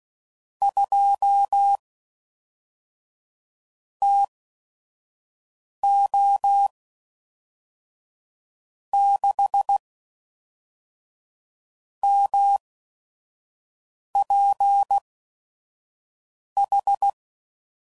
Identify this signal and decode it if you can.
Morse code, heard as a keyed tone, about 16 wpm, spaced as for 5 wpm, 800 Hz. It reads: 2 TO 6 MPH
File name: CW Code Groups @5WPM / Weather Sets / Weather 4 10